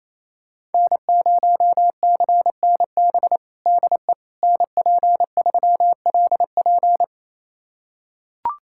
Morse code, heard as a keyed tone, about 27 wpm, 700 Hz. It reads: N0CNB DE NP3LP
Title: USA Callsign DE Callsign - 5 Characters - 28wpm